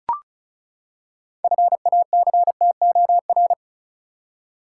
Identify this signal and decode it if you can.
Morse code, heard as a keyed tone, about 34 wpm, 700 Hz. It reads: FACTOR
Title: Six Letter Words - 35wpm